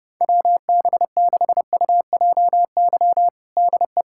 Morse code, heard as a keyed tone, about 32 wpm, 700 Hz. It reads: WB6UJY DE
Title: USA Callsign DE Callsign 30wpm